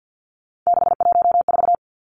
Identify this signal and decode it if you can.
Morse code, heard as a keyed tone, about 50 wpm, 700 Hz. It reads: 614
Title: Three Challenging Letter-Number Combinations 50wpm